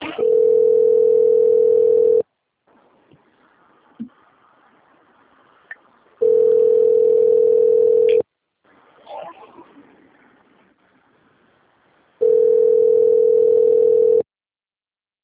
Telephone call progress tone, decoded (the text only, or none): ringback tone